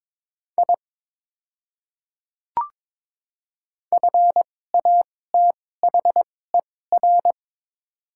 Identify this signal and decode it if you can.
Morse code, heard as a keyed tone, about 22 wpm, spaced as for 16 wpm, 700 Hz. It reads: I FATHER
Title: 2x Character Spacing - Top 300 Words - 22wpm